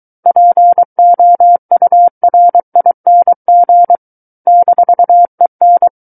Morse code, heard as a keyed tone, about 23 wpm, 700 Hz. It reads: POURING -EN
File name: Book One / 14-Chapter 14 - 23 WPM